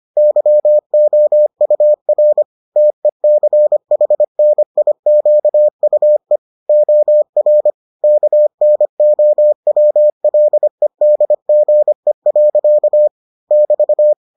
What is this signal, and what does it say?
Morse code, heard as a keyed tone, about 25 wpm, 600 Hz. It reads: YOUR TECHNIQUE OR KNOWLEDGE. <BT>